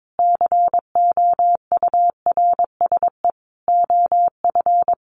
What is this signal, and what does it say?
Morse code, heard as a keyed tone, about 22 wpm, 700 Hz. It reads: COURSE OF
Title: Sentences from Top 1400 Words 22wpm